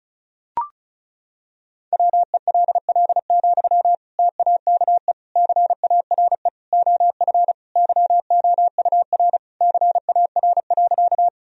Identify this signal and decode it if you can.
Morse code, heard as a keyed tone, about 34 wpm, 700 Hz. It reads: WELL, TAKE CARE OF YOUR CAR.
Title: Sentences from Top 1000 Words - Review Part 1 - 35wpm